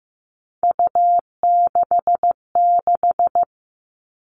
Morse code, heard as a keyed tone, about 15 wpm, 700 Hz. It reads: U66
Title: All Cut Numbers - 3-Digit 15wpm